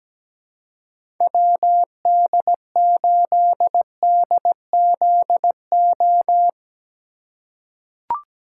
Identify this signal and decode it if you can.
Morse code, heard as a keyed tone, about 17 wpm, 700 Hz. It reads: WD8DZO
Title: USA Call Signs - 6 Characters - 4x Character Spacing - 17wpm